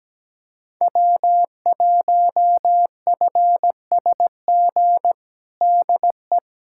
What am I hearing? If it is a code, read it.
Morse code, heard as a keyed tone, about 17 wpm, 700 Hz. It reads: W1FSG DE
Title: USA Callsign DE Callsign - 5 Characters - 17wpm